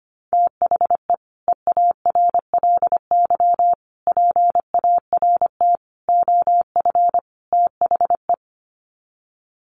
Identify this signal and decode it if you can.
Morse code, heard as a keyed tone, about 25 wpm, 700 Hz. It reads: THE EARLY PART OF THE
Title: Sets of 5 Words Encore2 25wpm